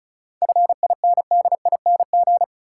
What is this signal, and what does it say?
Morse code, heard as a keyed tone, about 36 wpm, 700 Hz. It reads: FINDING
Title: Sentences from Top 1100 Words - New Word Drill - 35wpm